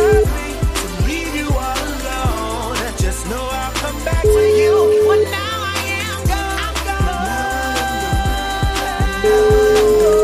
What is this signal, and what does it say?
Ringback tone (Europe, 425 Hz), with noise.